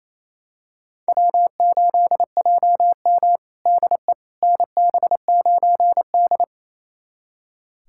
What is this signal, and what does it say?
Morse code, heard as a keyed tone, about 28 wpm, 700 Hz. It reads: W8JM DE NB9D
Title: USA Callsign DE Callsign - 4 Characters - 28wpm